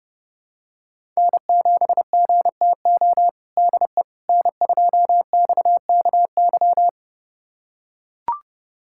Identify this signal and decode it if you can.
Morse code, heard as a keyed tone, about 32 wpm, 700 Hz. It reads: N7GTO DE N2XKY
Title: USA Callsign DE Callsign - 5 Characters - 30wpm